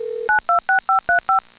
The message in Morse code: none